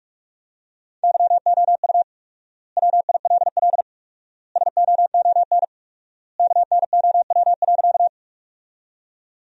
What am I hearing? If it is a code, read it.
Morse code, heard as a keyed tone, about 45 wpm, 700 Hz. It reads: YOU WILL SOON KNOW.
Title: 4x Word-Spacing - Sentences from Top 400 Words - 45wpm